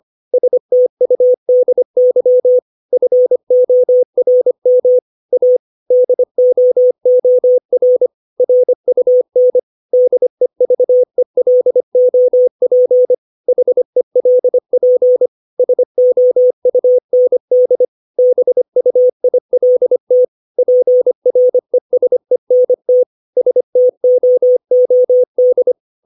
Morse code, heard as a keyed tone, about 25 wpm, 500 Hz. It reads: STUDY FORM A DOOR RUN DEVELOP HELP SOUND BUILT PRESENT STOOD
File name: 632 words / 632 Words 25WPM